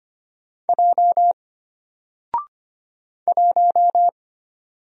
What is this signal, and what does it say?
Morse code, heard as a keyed tone, about 25 wpm, 700 Hz. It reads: J 1
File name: Single-Letter-Number-25wpm